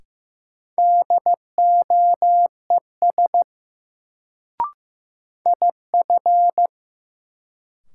Morse code, heard as a keyed tone, about 15 wpm, 700 Hz. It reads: DOES IF
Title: Sets of 2 Words - New Word Drill - 15wpm